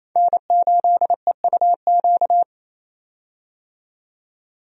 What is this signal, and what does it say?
Morse code, heard as a keyed tone, about 28 wpm, 700 Hz. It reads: N8EUQ
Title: USA Call Signs - 5 Characters - 28wpm